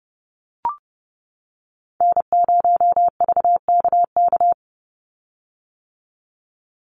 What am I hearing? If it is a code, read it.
Morse code, heard as a keyed tone, about 30 wpm, 700 Hz. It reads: N0VKK
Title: USA Call Signs 30wpm